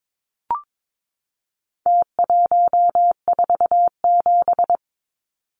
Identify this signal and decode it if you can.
Morse code, heard as a keyed tone, about 22 wpm, 700 Hz. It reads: T147
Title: Numbers - Contest Style - 4-Digit 22wpm